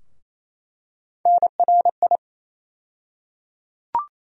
Morse code, heard as a keyed tone, about 28 wpm, 700 Hz. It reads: NRI
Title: USA Call Sign Postfixes - 3 Letters 28wpm